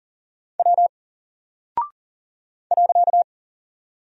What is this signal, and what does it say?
Morse code, heard as a keyed tone, about 40 wpm, 700 Hz. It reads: W .
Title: Single Character (Letter:Number:Punctuation:Prosign) 40wpm